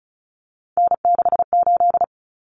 Morse code, heard as a keyed tone, about 34 wpm, 700 Hz. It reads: N68
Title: Common Cut Numbers - 3-Digit 35wpm